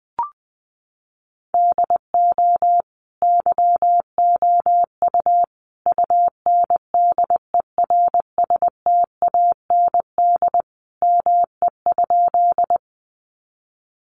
Morse code, heard as a keyed tone, about 20 wpm, 700 Hz. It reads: DO YOU UNDERSTAND ME?
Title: Sentences from Top 600 Words 20wpm